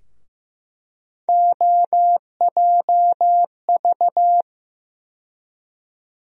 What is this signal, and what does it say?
Morse code, heard as a keyed tone, about 15 wpm, 700 Hz. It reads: OJV
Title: USA Call Sign Postfixes - 3 Letters 15wpm